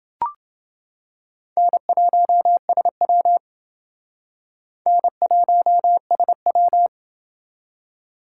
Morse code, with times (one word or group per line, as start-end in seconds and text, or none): 1.6-3.4 N1SW
4.9-6.9 N1SW